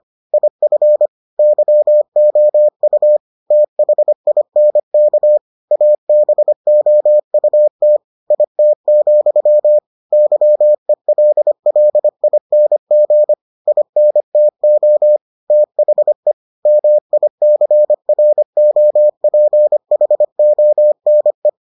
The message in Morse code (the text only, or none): IF YOU THINK ABOUT IT, YELLING INTO THE MICROPHONE